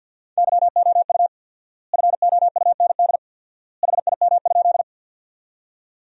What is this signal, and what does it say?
Morse code, heard as a keyed tone, about 50 wpm, 700 Hz. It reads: YOU FOUND HIM?
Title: 4x Word-Spacing - Sentences from Top 300 Words - 50wpm